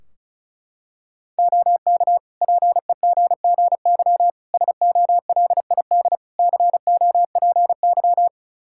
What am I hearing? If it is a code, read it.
Morse code, heard as a keyed tone, about 34 wpm, 700 Hz. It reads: OK PEGGY SOLID COPY
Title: Practice QSO - Exchange 3 - More Info - 35wpm